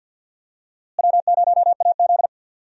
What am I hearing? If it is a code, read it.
Morse code, heard as a keyed tone, about 51 wpm, 700 Hz. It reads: W0AZ
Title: USA Call Signs - 4 Characters - Repeated 3x - 50wpm